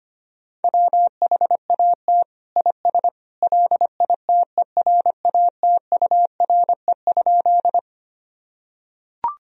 Morse code, heard as a keyed tone, about 25 wpm, 700 Hz. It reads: WHAT IS LITERATURE?
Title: Sentences from Top 1000 Words - Review Part 3 - 25wpm